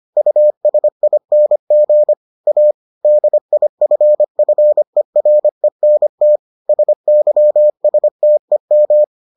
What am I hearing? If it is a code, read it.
Morse code, heard as a keyed tone, about 25 wpm, 600 Hz. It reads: USING A DIFFERENT SYSTEM